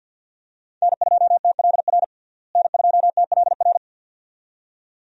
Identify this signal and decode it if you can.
Morse code, heard as a keyed tone, about 53 wpm, 700 Hz. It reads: N2TLR N2TLR
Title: USA Call Signs - 5 Characters - Repeated 2x - 50wpm